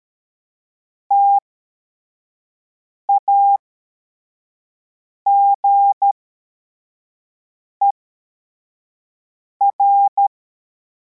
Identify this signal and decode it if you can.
Morse code, heard as a keyed tone, about 13 wpm, spaced as for 4 wpm, 800 Hz. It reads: TAGER
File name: formation / cw / Text43